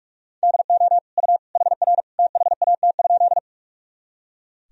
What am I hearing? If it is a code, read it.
Morse code, heard as a keyed tone, about 44 wpm, 700 Hz. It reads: DO U HR THAT?
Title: Sentences from Top 300 Words and Abbreviations 45wpm